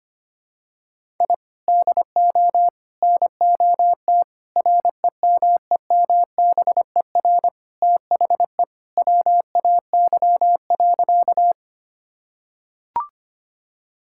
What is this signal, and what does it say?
Morse code, heard as a keyed tone, about 25 wpm, 700 Hz. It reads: I DO NOT REMEMBER THE WAY.
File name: practice / 25wpm / Sentences with 4-Word Phrases - 25wpm